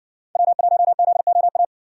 Morse code, heard as a keyed tone, about 60 wpm, 700 Hz. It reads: W1ZOA